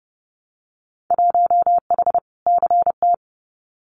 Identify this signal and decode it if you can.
Morse code, heard as a keyed tone, about 30 wpm, 700 Hz. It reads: 1H CT
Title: Winter Field Day Exchanges 30wpm